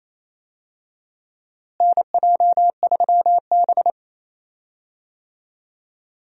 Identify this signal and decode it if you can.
Morse code, heard as a keyed tone, about 28 wpm, 700 Hz. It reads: NJ3B
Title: USA Call Signs - 4 Characters - Repeated 2x - 28wpm